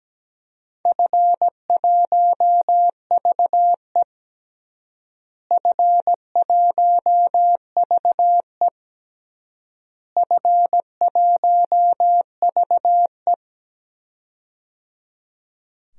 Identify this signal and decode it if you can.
Morse code, heard as a keyed tone, about 17 wpm, 700 Hz. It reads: F1VE F1VE F1VE
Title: DXCC-100 Call Signs - 4 Characters - Repeated 3x - 17wpm